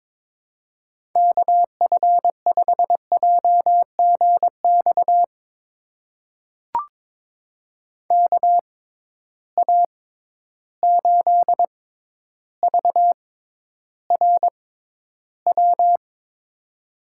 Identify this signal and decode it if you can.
Morse code, heard as a keyed tone, about 22 wpm, spaced as for 11 wpm, 700 Hz. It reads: KF5JGX KA8VRW
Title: USA Call Signs - 6 Characters - 6x Character Spacing - 22wpm